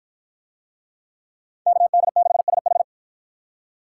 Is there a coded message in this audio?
Yes: Morse code from 1.7 to 2.8 s, heard as a keyed tone, about 48 wpm, 700 Hz. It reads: KD6SH